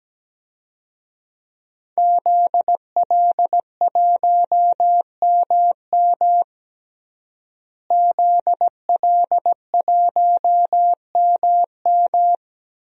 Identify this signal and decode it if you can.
Morse code, heard as a keyed tone, about 17 wpm, 700 Hz. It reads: ZL1MM ZL1MM
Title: DXCC-100 Call Signs - 5 Characters - Repeated 3x - 17wpm